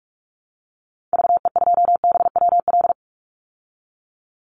Morse code, heard as a keyed tone, about 44 wpm, 700 Hz. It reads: VE2BWL
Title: Call Signs Repeated 2x - 45wpm